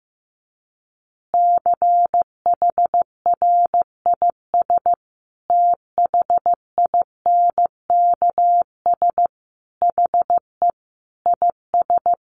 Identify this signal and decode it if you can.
Morse code, heard as a keyed tone, about 15 wpm, 700 Hz. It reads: CHRIS THINKS HE IS
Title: Additional Easy Phrases 15wpm